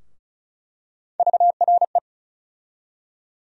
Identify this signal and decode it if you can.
Morse code, heard as a keyed tone, about 36 wpm, 700 Hz. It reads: VRE
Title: USA Call Sign Postfixes - 3 Letters 35wpm